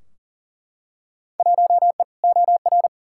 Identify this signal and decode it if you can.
Morse code, heard as a keyed tone, about 40 wpm, 700 Hz. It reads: 1E OR
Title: Field Day Exchanges Encore - 40wpm